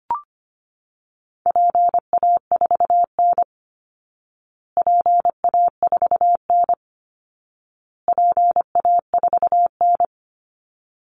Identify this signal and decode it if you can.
Morse code, heard as a keyed tone, about 25 wpm, 700 Hz. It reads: PA4N PA4N PA4N
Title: Call Signs Repeated 2x - 25wpm